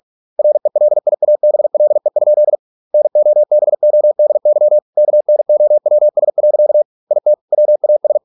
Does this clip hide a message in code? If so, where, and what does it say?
Yes: Morse code from 0.4 to 8.2 s, heard as a keyed tone, about 46 wpm, 600 Hz. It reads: RELIABLE? NOBODY KNOWS. IT WAS